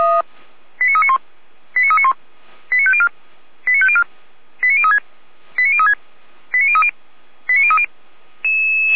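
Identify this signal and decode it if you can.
Five-tone selective call (ZVEI), heard as quick runs of five tones, tones of about 70 ms. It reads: 89391 89391 89504 89504 89036 89036 89030 89030
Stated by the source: ZVEI five-tone selective call 89042 received on an SDR.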